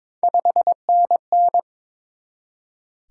Morse code, heard as a keyed tone, about 22 wpm, 700 Hz. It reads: 5NN
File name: Top 20 QSO Elements - 22wpm